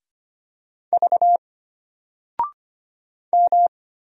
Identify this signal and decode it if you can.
Morse code, heard as a keyed tone, about 25 wpm, 700 Hz. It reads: V M
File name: Single-Letter-Number-25wpm